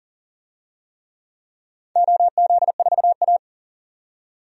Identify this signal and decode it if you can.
Morse code, heard as a keyed tone, about 37 wpm, 700 Hz. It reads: OZ4A
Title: DXCC-100 Call Signs - 4 Characters - 40wpm